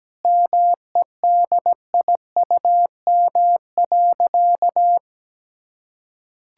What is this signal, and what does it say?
Morse code, heard as a keyed tone, about 17 wpm, 700 Hz. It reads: MEDIUM.
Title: Sentences from Top 1000 Words - Review Part 6 - 17wpm